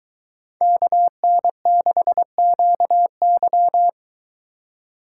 Morse code, heard as a keyed tone, about 22 wpm, 700 Hz. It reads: KN6QY